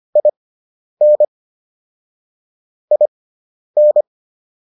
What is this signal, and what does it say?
Morse code, heard as a keyed tone, about 25 wpm, spaced as for 10 wpm, 600 Hz. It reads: IN IN